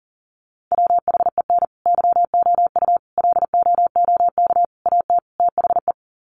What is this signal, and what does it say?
Morse code, heard as a keyed tone, about 40 wpm, 700 Hz. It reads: WHEN YOU LOOK AT THE
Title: Sets of 5 Words 40wpm